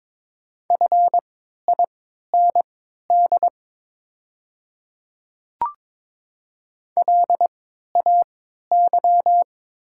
Morse code, heard as a keyed tone, about 22 wpm, spaced as for 13 wpm, 700 Hz. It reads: FIND LAY
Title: 3x Character Spacing - Top 500 Words - 22wpm